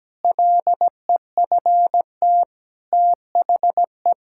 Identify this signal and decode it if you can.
Morse code, heard as a keyed tone, about 17 wpm, 700 Hz. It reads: LEFT THE
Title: Sentences from Top 1000 Words - Review Part 5 - 17wpm